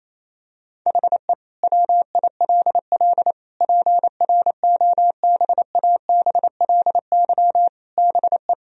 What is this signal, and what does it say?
Morse code, heard as a keyed tone, about 28 wpm, 700 Hz. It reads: HE WILL PROBABLY BE
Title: Sentences from Top 1000 Words - Review Part 4 - 28wpm